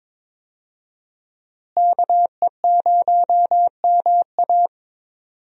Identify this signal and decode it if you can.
Morse code, heard as a keyed tone, about 22 wpm, 700 Hz. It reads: KE0MA
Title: USA Call Signs - 5 Characters - Repeated 3x - 22wpm